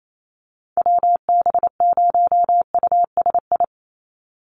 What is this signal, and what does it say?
Morse code, heard as a keyed tone, about 28 wpm, 700 Hz. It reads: WB0USI
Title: NA SOTA Activator Callsigns 28wpm